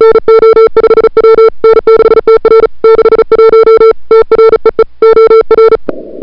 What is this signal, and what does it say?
Morse code, heard as a keyed tone, about 35 wpm, 440 Hz. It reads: NO5W N6TR 61 TREE OR
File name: CWSprint / February 2007 CW Sprint / 01-2007 CW Sprint Hour1